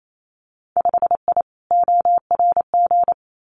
Morse code, heard as a keyed tone, about 28 wpm, 700 Hz. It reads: 5I ORG
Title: Winter Field Day Exchanges Encore 28wpm